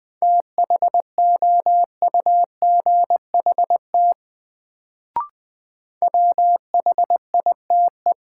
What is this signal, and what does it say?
Morse code, heard as a keyed tone, about 20 wpm, 700 Hz. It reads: THOUGHT WHITE
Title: Sentences from Top 300 Words - New Word Drill - 20wpm